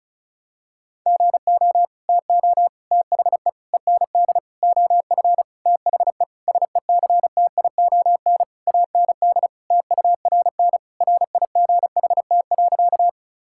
Morse code, heard as a keyed tone, about 36 wpm, 700 Hz. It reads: GO TO THE END OF THE SECTION AND TURN RIGHT.
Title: Sentences from Top 1000 Words - Review Part 5 - 35wpm